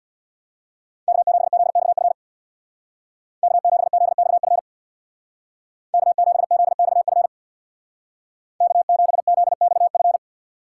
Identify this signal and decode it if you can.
Morse code, heard as a keyed tone, about 61 wpm, 700 Hz. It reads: K7ZXF K7ZXF K7ZXF K7ZXF
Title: Speed Racing - USA Call Signs 50wpm